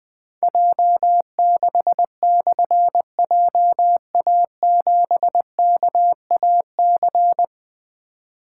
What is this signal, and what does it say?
Morse code, heard as a keyed tone, about 20 wpm, 700 Hz. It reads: J6/JA7KAC
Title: Difficult Call Signs 20wpm